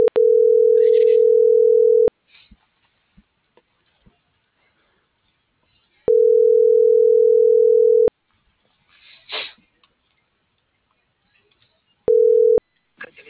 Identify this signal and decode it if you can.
Ringback tone (North America), 440 + 480 Hz.